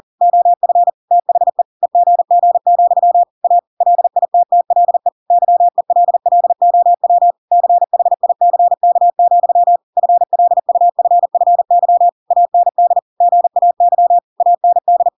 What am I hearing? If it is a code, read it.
Morse code, heard as a keyed tone, about 40 wpm, 700 Hz. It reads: OF THE EGG, A LITTLE YELLOW CHICK, FLUFFY AND GAY AND
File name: Pinocchio / Pinocchio ch5 40 40 ik1hge